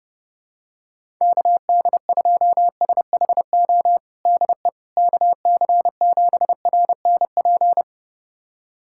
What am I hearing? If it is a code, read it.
Morse code, heard as a keyed tone, about 30 wpm, 700 Hz. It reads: KD2SHO DE KC7RNP
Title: USA Callsign DE Callsign - 6 Characters - 30wpm